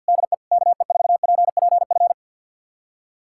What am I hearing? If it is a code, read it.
Morse code, heard as a keyed tone, about 50 wpm, 700 Hz. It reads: DE KE4PPF
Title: USA Callsign DE Callsign 50wpm